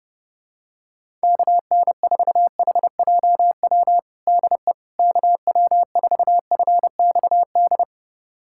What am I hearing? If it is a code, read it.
Morse code, heard as a keyed tone, about 30 wpm, 700 Hz. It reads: KN4HJW DE KW4FXD
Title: USA Callsign DE Callsign - 6 Characters - 30wpm